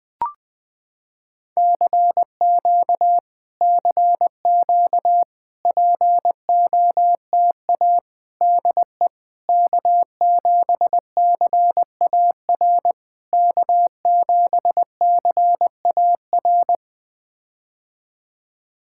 Morse code, heard as a keyed tone, about 20 wpm, 700 Hz. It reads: CQ CQ POTA DE K7CAR K7CAR
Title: POTA QSO Practice - 20wpm